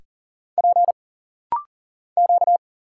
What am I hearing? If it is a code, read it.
Morse code, heard as a keyed tone, about 40 wpm, 700 Hz. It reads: P Q
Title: Single Letters - Rapid-Fire 40wpm